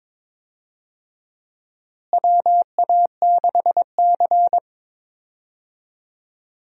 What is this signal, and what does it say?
Morse code, heard as a keyed tone, about 22 wpm, 700 Hz. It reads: WA6C
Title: USA Call Signs - 4 Characters - Repeated 3x - 22wpm